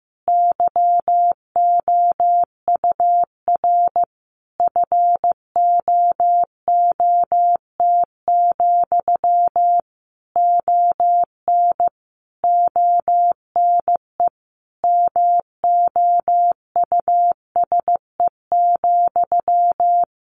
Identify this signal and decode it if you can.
Morse code, heard as a keyed tone, about 15 wpm, 700 Hz. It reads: YOUR FOOT, ON ONE MOUSE,
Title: A Sound Of Thunder 15wpm